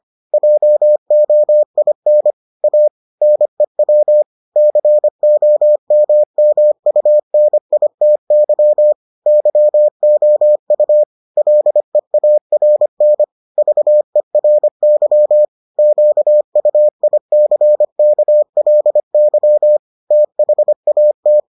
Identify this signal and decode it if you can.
Morse code, heard as a keyed tone, about 25 wpm, 600 Hz. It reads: JOIN A NEW COMMUNITY YOU LEARN VERY QUICKLY THAT